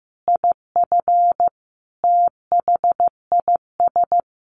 Morse code, heard as a keyed tone, about 15 wpm, 700 Hz. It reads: IF THIS
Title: A Sound Of Thunder 15wpm